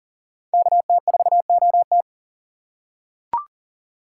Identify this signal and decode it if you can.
Morse code, heard as a keyed tone, about 40 wpm, 700 Hz. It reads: KT4OT
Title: USA Call Signs - 5 Characters - 4x Character Spacing - 40wpm